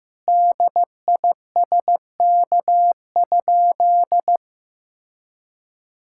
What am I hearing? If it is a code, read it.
Morse code, heard as a keyed tone, about 16 wpm, 700 Hz. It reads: DISK?
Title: Sentences from Top 1000 Words - Review Part 6 - 15wpm